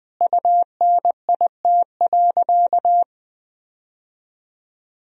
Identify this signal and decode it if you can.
Morse code, heard as a keyed tone, about 21 wpm, 700 Hz. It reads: UNIT.
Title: Sentences from Top 1000 Words - Review Part 2 - 20wpm